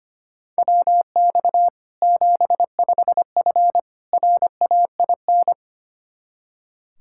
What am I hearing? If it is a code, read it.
Morse code, heard as a keyed tone, about 25 wpm, 700 Hz. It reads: WX 75F RAIN
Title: Practice QSO - Exchange 2 - Info - 25wpm